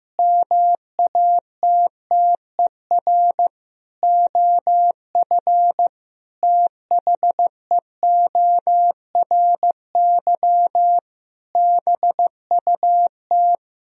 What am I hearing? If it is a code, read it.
Morse code, heard as a keyed tone, about 15 wpm, 700 Hz. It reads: MATTER OF THEORY BUT
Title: Sentences from Top 1000 Words - Review Part 4 - 15wpm